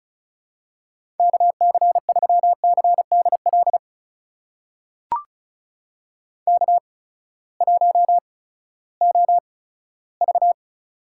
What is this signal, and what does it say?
Morse code, heard as a keyed tone, about 36 wpm, spaced as for 18 wpm, 700 Hz. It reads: KC3CDL K1OV
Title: USA Call Signs - 8x Character Spacing - 35wpm